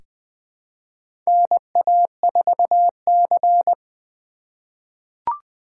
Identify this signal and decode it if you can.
Morse code, heard as a keyed tone, about 20 wpm, 700 Hz. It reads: NA4C
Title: K1USN SST QSO Practice - 20wpm